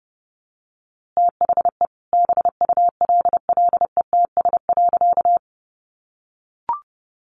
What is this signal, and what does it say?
Morse code, heard as a keyed tone, about 30 wpm, 700 Hz. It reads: THE BULLETS.
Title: A Sound Of Thunder 30wpm